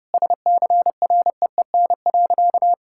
Morse code, heard as a keyed tone, about 30 wpm, 700 Hz. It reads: SCREEN.
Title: Sentences from Top 1000 Words - Review Part 6 - 30wpm